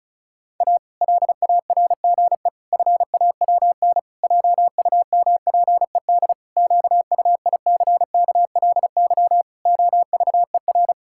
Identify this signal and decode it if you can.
Morse code, heard as a keyed tone, about 36 wpm, 700 Hz. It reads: A LARGE FAWN JUMPED QUICKLY OVER
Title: Pangram Challenge v2 - 35wpm